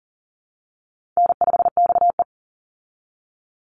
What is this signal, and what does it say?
Morse code, heard as a keyed tone, about 40 wpm, 700 Hz. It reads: N5XE
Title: Easy Call Signs Repeated 2x - 40wpm